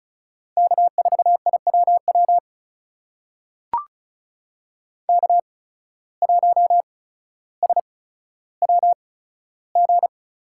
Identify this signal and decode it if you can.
Morse code, heard as a keyed tone, about 36 wpm, spaced as for 15 wpm, 700 Hz. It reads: K4IWW K1SWG
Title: USA Call Signs - 5 Characters - 8x Character Spacing - 35wpm